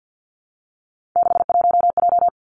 Morse code, heard as a keyed tone, about 50 wpm, 700 Hz. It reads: 61J
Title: Three Challenging Letter-Number Combinations 50wpm